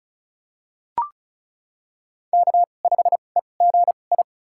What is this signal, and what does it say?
Morse code, heard as a keyed tone, about 36 wpm, spaced as for 25 wpm, 700 Hz. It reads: K5EGI
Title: USA Call Signs - 5 Characters - 2x Character Spacing - 35wpm